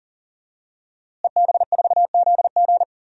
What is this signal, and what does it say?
Morse code, heard as a keyed tone, about 40 wpm, 700 Hz. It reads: EB4ZG
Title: DXCC-100 Call Signs - 5 Characters - Repeated 3x - 40wpm